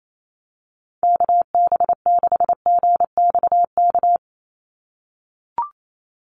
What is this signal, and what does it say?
Morse code, heard as a keyed tone, about 28 wpm, 700 Hz. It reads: KB6GXK
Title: USA Call Signs 28wpm